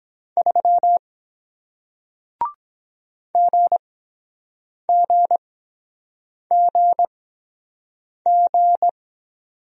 Morse code, heard as a keyed tone, about 23 wpm, spaced as for 7 wpm, 700 Hz. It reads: 3 GGGG